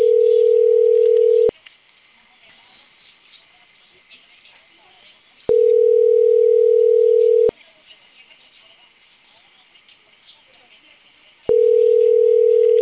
Ringback tone (North America), 440 + 480 Hz.
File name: CENTRAL CAEA / out agcaea02 - .39548